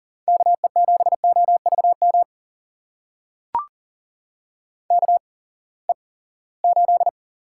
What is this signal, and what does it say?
Morse code, heard as a keyed tone, about 40 wpm, spaced as for 23 wpm, 700 Hz. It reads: KE7OVM KE8
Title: USA Call Signs - 6 Characters - 8x Character Spacing - 40wpm